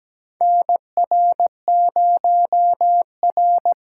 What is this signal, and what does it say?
Morse code, heard as a keyed tone, about 17 wpm, 700 Hz. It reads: NR0R
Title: USA Callsign DE Callsign - 4 Characters - 17wpm